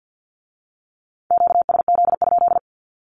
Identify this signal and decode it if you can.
Morse code, heard as a keyed tone, about 50 wpm, 700 Hz. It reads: QSZ?
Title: Q-Codes with Spoken Meaning - 50wpm